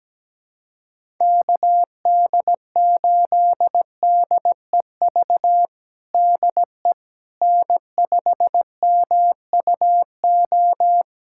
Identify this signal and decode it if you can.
Morse code, heard as a keyed tone, about 17 wpm, 700 Hz. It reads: KD8DEV DE N5MUO
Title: USA Callsign DE Callsign 17wpm